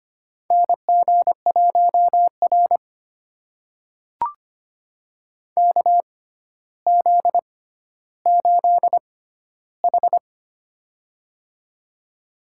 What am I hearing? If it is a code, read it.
Morse code, heard as a keyed tone, about 25 wpm, spaced as for 13 wpm, 700 Hz. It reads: NG1R KZ8H
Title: USA Call Signs - 4 Characters - 6x Character Spacing - 25wpm